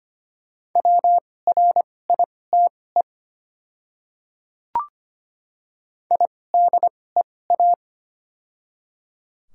Morse code, heard as a keyed tone, about 25 wpm, spaced as for 18 wpm, 700 Hz. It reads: WRITE IDEA
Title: 2x Character Spacing - Top 500 Words - 25wpm